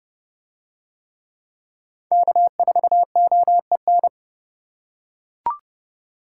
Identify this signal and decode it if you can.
Morse code, heard as a keyed tone, about 30 wpm, 700 Hz. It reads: K4OEN